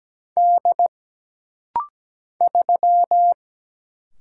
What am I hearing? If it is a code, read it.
Morse code, heard as a keyed tone, about 17 wpm, 700 Hz. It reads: D 3
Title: Single Letter-Number Alt - Rapid-Fire 17wpm